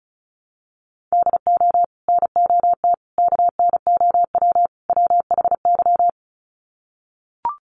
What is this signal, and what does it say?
Morse code, heard as a keyed tone, about 34 wpm, 700 Hz. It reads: DO NOT KNOW WHY
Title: Sets of 4 Words 35wpm